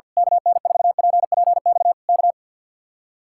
Morse code, heard as a keyed tone, about 50 wpm, 700 Hz. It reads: KN4PPX K
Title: Speed-Racing Common QSO Phrases Encore - 50wpm